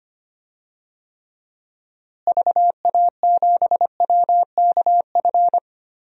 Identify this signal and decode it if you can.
Morse code, heard as a keyed tone, about 25 wpm, 700 Hz. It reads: VA7WKF
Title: DXCC-100 Call Signs - 6 Characters - Repeated 3x - 25wpm